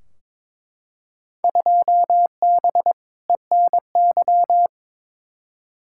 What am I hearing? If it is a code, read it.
Morse code, heard as a keyed tone, about 22 wpm, 700 Hz. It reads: 2B ENY
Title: Field Day Exchanges Encore - 22wpm